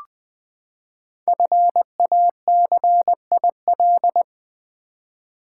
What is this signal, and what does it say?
Morse code, heard as a keyed tone, about 20 wpm, 700 Hz. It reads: FACIL
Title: Build A Word - 5 Letters At A Time - 10-Letter Words - 20wpm